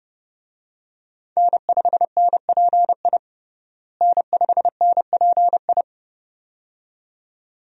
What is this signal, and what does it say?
Morse code, heard as a keyed tone, about 32 wpm, 700 Hz. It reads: N5NPI N5NPI
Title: USA Call Signs - 5 Characters - Repeated 2x - 30wpm